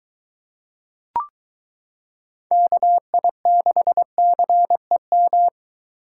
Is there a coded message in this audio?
Yes: Morse code from 2.5 to 5.5 s, heard as a keyed tone, about 23 wpm, 700 Hz. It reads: KI6CEM